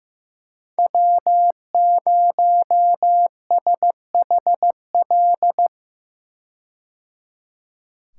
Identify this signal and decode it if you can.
Morse code, heard as a keyed tone, about 15 wpm, 700 Hz. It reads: W0SHL
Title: K1USN SST QSO Practice - 15wpm